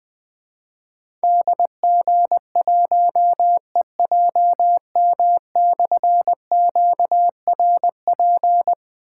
Morse code, heard as a keyed tone, about 20 wpm, 700 Hz. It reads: DG1EJM/QRP
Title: Difficult Call Signs 20wpm